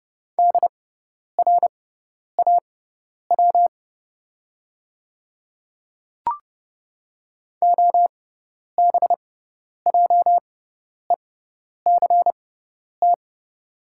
Morse code, heard as a keyed tone, about 30 wpm, spaced as for 10 wpm, 690 Hz. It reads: DRAW OBJECT
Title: 6x Character Spacing - Top 500 Words - 30wpm